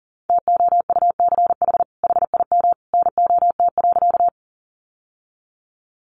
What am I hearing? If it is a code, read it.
Morse code, heard as a keyed tone, about 40 wpm, 700 Hz. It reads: TOUCH HIM NOT.
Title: Sentences from Top 1200 Words 40wpm